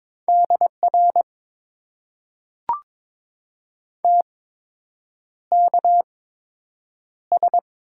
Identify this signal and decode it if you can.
Morse code, heard as a keyed tone, about 23 wpm, spaced as for 8 wpm, 700 Hz. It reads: DR TKS
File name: Top 90 QSO Elements - 8x Character Spacing - 22wpm